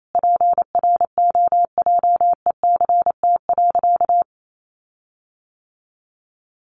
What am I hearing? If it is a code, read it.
Morse code, heard as a keyed tone, about 28 wpm, 700 Hz. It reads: PROJECT.
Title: Sentences from Top 1400 Words 28wpm